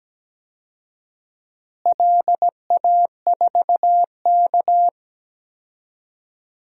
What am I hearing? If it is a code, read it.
Morse code, heard as a keyed tone, about 17 wpm, 700 Hz. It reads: LA4K
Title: DXCC-100 Call Signs - 4 Characters - Repeated 2x - 17wpm